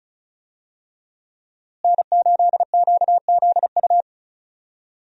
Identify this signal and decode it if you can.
Morse code, heard as a keyed tone, about 34 wpm, 700 Hz. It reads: N8QZU
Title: USA Call Signs - 5 Characters - Repeated 3x - 35wpm